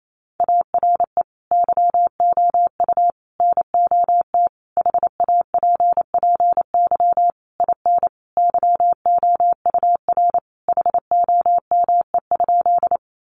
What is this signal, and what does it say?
Morse code, heard as a keyed tone, about 28 wpm, 700 Hz. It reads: ARE YOU NOT HAPPY IN YOUR HOME?
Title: Sentences from Top 1000 Words 28wpm